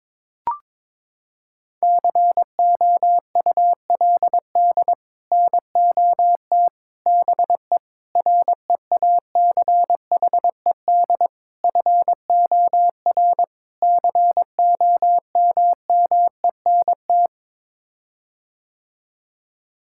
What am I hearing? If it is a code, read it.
Morse code, heard as a keyed tone, about 22 wpm, 700 Hz. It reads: COULD NOT BE REACHED FOR COMMENT
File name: practice / 22wpm / Sets of 6 Words - 22wpm